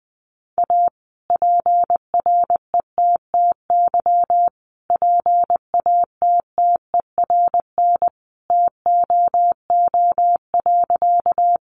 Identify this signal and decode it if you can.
Morse code, heard as a keyed tone, about 20 wpm, 700 Hz. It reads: A PRETTY PATTERN TOO.
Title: Sentences from Top 1000 Words 20wpm